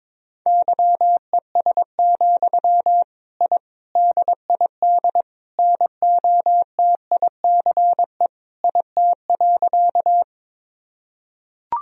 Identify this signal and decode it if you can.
Morse code, heard as a keyed tone, about 22 wpm, 700 Hz. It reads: YES, I DID NOTICE IT.
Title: Sentences from Top 1000 Words - Review Part 2 - 22wpm